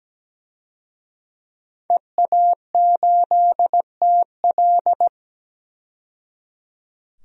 Morse code, heard as a keyed tone, about 17 wpm, 700 Hz. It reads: EA8TL
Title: DXCC-100 Call Signs - 5 Characters - 17wpm